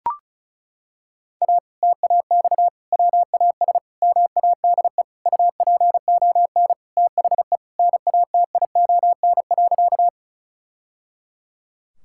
Morse code, heard as a keyed tone, about 36 wpm, 700 Hz. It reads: A TAX WAS MADE UPON THE NATION.
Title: Sentences from Top 1000 Words - Review Part 6 - 35wpm